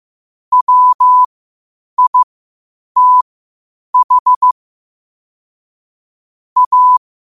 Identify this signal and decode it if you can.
Morse code, heard as a keyed tone, about 15 wpm, spaced as for 8 wpm, 1000 Hz. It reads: WITH A
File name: qotdfiles / 8wpm